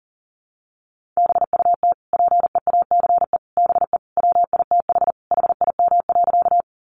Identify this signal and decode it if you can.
Morse code, heard as a keyed tone, about 37 wpm, 700 Hz. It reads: BUT PEACE BE WITH HIM.
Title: Sentences from Top 1500 Words 40wpm